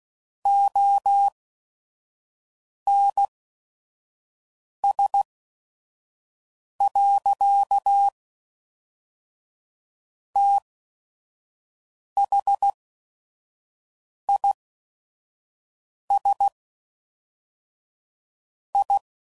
Morse code, heard as a keyed tone, about 16 wpm, spaced as for 5 wpm, 800 Hz. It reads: ONS. THIS I
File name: CW Code Groups @5WPM / Technical Subjects / AND Gate